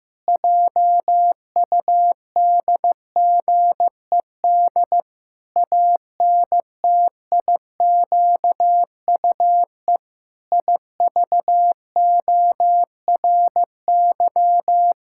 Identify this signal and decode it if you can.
Morse code, heard as a keyed tone, about 16 wpm, 700 Hz. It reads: JUDGED ANTIQUE IVORY
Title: Pangram Challenge v2 - 15wpm